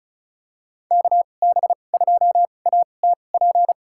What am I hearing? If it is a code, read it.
Morse code, heard as a keyed tone, about 34 wpm, spaced as for 25 wpm, 700 Hz. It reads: KB2ATP
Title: USA Call Signs - 2x Character Spacing - 35wpm